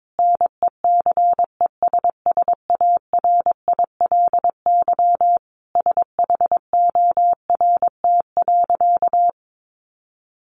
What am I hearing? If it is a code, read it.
Morse code, heard as a keyed tone, about 22 wpm, 700 Hz. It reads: NECESSARILY SHORT.
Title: Sentences from Top 1400 Words 22wpm